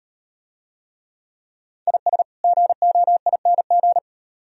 Morse code, heard as a keyed tone, about 36 wpm, 700 Hz. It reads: IS GOING